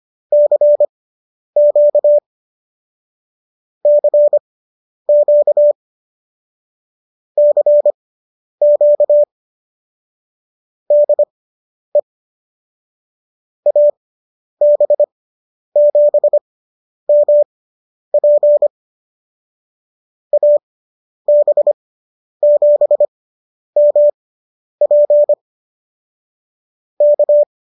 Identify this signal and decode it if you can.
Morse code, heard as a keyed tone, about 25 wpm, spaced as for 10 wpm, 590 Hz. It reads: CQ CQ CQ DE AB7MP AB7MP K